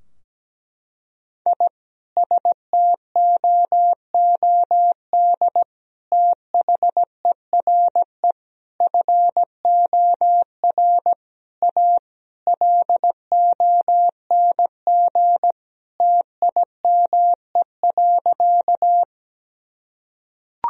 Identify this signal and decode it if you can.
Morse code, heard as a keyed tone, about 17 wpm, 700 Hz. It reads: I STOOD THERE FOR A LONG TIME.
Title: Sentences from Top 1000 Words - Review Part 2 - 17wpm